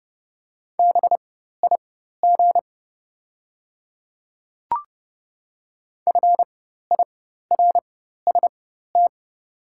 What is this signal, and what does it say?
Morse code, heard as a keyed tone, about 30 wpm, spaced as for 14 wpm, 700 Hz. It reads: BIG FIRST
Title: 4x Character Spacing - Top 200 Words - 30wpm